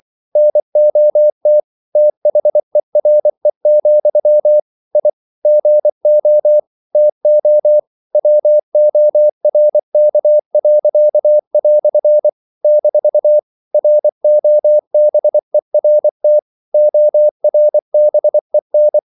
Morse code, heard as a keyed tone, about 24 wpm, 600 Hz. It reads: NOT THERE, I GO TO WORK." - ROBERT ORBEN
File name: CW / Practice / EnglishQuotations0324